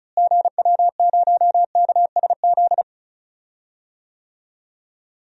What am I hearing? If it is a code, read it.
Morse code, heard as a keyed tone, about 36 wpm, 700 Hz. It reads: GW0KSZ
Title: DXCC-100 Call Signs - 6 Characters - Repeated 3x - 35wpm